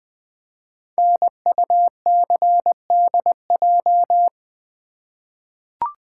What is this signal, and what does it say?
Morse code, heard as a keyed tone, about 20 wpm, 700 Hz. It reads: NUCDJ
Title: Code Groups - 5 Letters - 20wpm